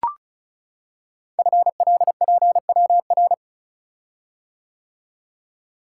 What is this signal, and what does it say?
Morse code, heard as a keyed tone, about 36 wpm, 690 Hz. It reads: FLPWR
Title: Code Groups - 5 Letters - 35wpm